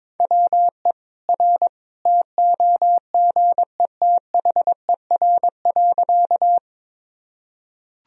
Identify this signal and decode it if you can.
Morse code, heard as a keyed tone, about 22 wpm, 700 Hz. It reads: WE R TOGETHER.
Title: Sentences from Top 300 Words and Abbreviations 22wpm